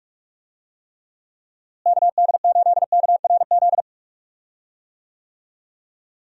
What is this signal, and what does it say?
Morse code, heard as a keyed tone, about 41 wpm, 700 Hz. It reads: KD8KRZ
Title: USA Call Signs - 6 Characters - Repeated 2x - 45wpm